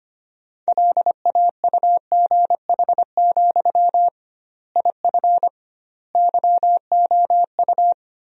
Morse code, heard as a keyed tone, about 25 wpm, 700 Hz. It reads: LAUGH, IF YOU
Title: 2x Word-Spacing - Sentences from Top 500 Words - 25wpm